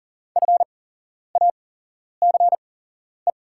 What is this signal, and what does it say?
Morse code, heard as a keyed tone, about 40 wpm, spaced as for 11 wpm, 700 Hz. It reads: FACE
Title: 8x Character Spacing - Top 500 Words - 40wpm